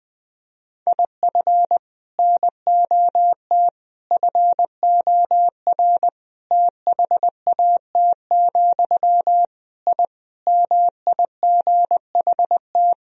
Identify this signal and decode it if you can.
Morse code, heard as a keyed tone, about 20 wpm, 700 Hz. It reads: IF NOT FOR THAT, I MIGHT
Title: Sentences from Top 1000 Words - Review Part 3 - 20wpm